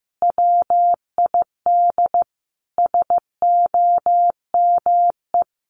Morse code, heard as a keyed tone, about 15 wpm, 700 Hz. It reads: WID SOME
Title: Common QSO Phrases 15wpm